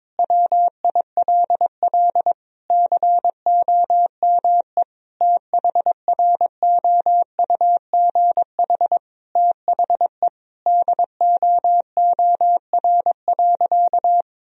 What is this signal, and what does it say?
Morse code, heard as a keyed tone, about 22 wpm, 700 Hz. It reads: WILL COME THROUGH THE DOOR.
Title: Sentences from Top 1000 Words - Review Part 1 - 22wpm